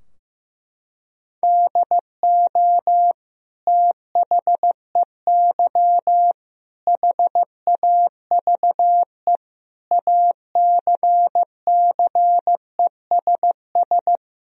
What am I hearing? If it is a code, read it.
Morse code, heard as a keyed tone, about 15 wpm, 700 Hz. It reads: DO THEY HAVE ACCESS
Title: Sentences from Top 1000 Words - Review Part 4 - 15wpm